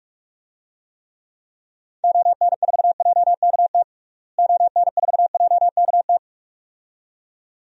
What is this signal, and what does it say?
Morse code, heard as a keyed tone, about 46 wpm, 700 Hz. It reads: ON4JKT ON4JKT
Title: DXCC-100 Call Signs - 6 Characters - Repeated 2x - 45wpm